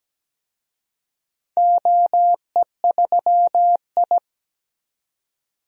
Morse code, heard as a keyed tone, about 17 wpm, 700 Hz. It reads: OE3I
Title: DXCC-100 Call Signs - 4 Characters - 17wpm